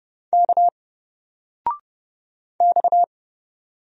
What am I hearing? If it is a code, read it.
Morse code, heard as a keyed tone, about 31 wpm, 700 Hz. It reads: K X
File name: Single-Letter-Number-30wpm